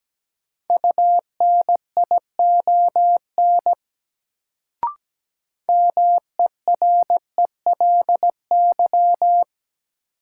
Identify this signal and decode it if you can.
Morse code, heard as a keyed tone, about 17 wpm, 700 Hz. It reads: UNION MERELY
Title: Sentences from Top 1400 Words - New Word Drill - 17wpm